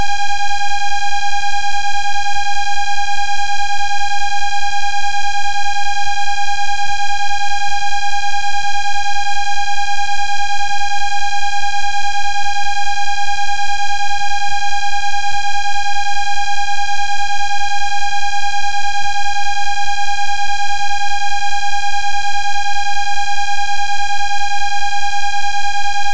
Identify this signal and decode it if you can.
There is no code.